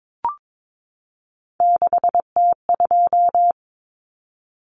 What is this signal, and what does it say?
Morse code, heard as a keyed tone, about 22 wpm, 700 Hz. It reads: 6T2
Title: Common Cut Numbers - 3-Digit 22wpm